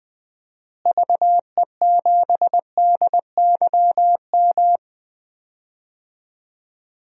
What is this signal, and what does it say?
Morse code, heard as a keyed tone, about 20 wpm, 700 Hz. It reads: VE7DYM
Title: DXCC-100 Call Signs - 6 Characters - Repeated 2x - 20wpm